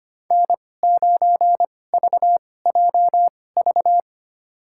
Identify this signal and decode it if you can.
Morse code, heard as a keyed tone, about 25 wpm, spaced as for 18 wpm, 700 Hz. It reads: N9VJV
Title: USA Call Signs - 5 Characters - 2x Character Spacing - 25wpm